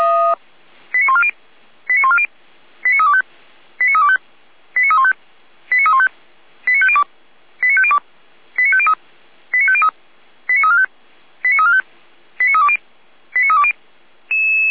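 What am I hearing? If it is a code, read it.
Five-tone selective call (ZVEI), heard as quick runs of five tones, tones of about 70 ms. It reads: 89140 89140 89325 89325 89315 89315 89502 89502 89503 89503 89345 89345 89320 89320
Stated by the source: ZVEI five-tone call received by SDR, from feuerwehr-orlamuende.org audio archive.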